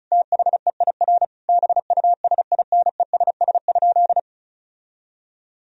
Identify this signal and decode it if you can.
Morse code, heard as a keyed tone, about 34 wpm, 700 Hz. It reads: THEIR BUSINESS?
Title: Sentences from Top 1000 Words - Review Part 4 - 35wpm